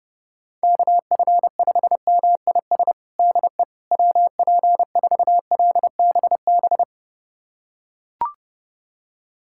Morse code, heard as a keyed tone, about 30 wpm, 700 Hz. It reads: KF5MIS DE WP4LBB
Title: USA Callsign DE Callsign - 6 Characters - 30wpm